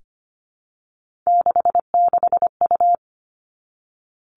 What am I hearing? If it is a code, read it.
Morse code, heard as a keyed tone, about 25 wpm, 700 Hz. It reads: 66U
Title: Three Challenging Letter-Number Combinations 25wpm